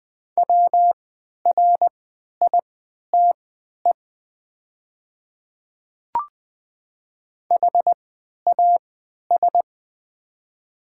Morse code, heard as a keyed tone, about 20 wpm, spaced as for 11 wpm, 700 Hz. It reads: WRITE HAS
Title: 3x Character Spacing - Top 100 Words - 20wpm